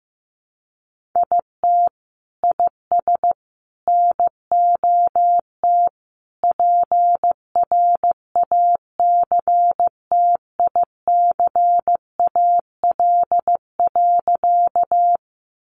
Morse code, heard as a keyed tone, about 15 wpm, 700 Hz. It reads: IT IS NOT PRACTICAL.
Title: Sentences from Top 1400 Words 15wpm